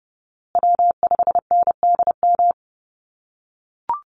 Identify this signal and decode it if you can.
Morse code, heard as a keyed tone, about 30 wpm, 700 Hz. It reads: W5NDM
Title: USA Call Signs Repeated 2x 30wpm